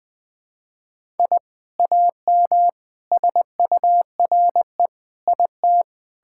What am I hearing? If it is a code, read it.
Morse code, heard as a keyed tone, about 20 wpm, 700 Hz. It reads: I AM SURE IT
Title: Sentences from Top 1000 Words - Review Part 5 - 20wpm